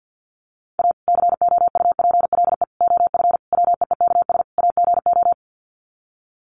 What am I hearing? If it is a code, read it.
Morse code, heard as a keyed tone, about 50 wpm, 700 Hz. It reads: A COUPLE OF WEEKS AGO
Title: Sets of 5 Words 50wpm